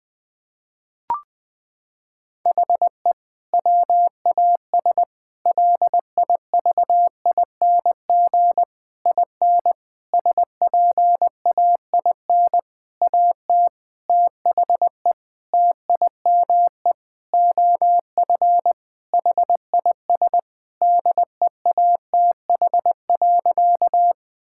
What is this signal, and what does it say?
Morse code, heard as a keyed tone, about 20 wpm, 700 Hz. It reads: HE WAS LIVING IN SPAIN AT THE TIME OF HIS DEATH.
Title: Sentences with 6-Word Phrases - 20wpm